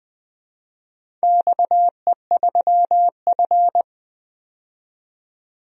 Morse code, heard as a keyed tone, about 20 wpm, 700 Hz. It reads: XE3F
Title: DXCC-100 Call Signs - 4 Characters - 20wpm